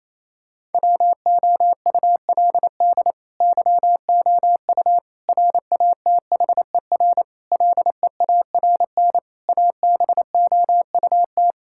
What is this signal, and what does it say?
Morse code, heard as a keyed tone, about 28 wpm, 700 Hz. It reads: WOULD YOU RATHER LEARN ABOUT
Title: Sentences from Top 1000 Words - Review Part 6 - 28wpm